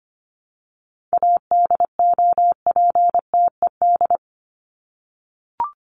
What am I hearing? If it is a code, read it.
Morse code, heard as a keyed tone, about 24 wpm, 700 Hz. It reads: ADOPTED
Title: ED Suffix - Top 100 Words with Base Words 25wpm